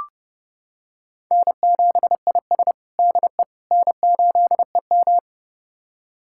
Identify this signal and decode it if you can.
Morse code, heard as a keyed tone, about 30 wpm, 700 Hz. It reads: N7IS DE N8EM
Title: USA Callsign DE Callsign - 4 Characters - 30wpm